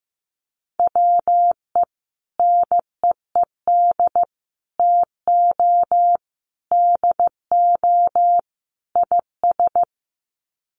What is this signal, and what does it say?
Morse code, heard as a keyed tone, about 15 wpm, 700 Hz. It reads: WE NEED TO DO IS
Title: Sets of 5 Words Encore2 15wpm